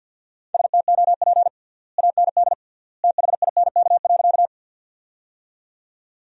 Morse code, heard as a keyed tone, about 52 wpm, 700 Hz. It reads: STOP AND THINK.
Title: 3x Word-Spacing - Sentences from Top 300 Words - 50wpm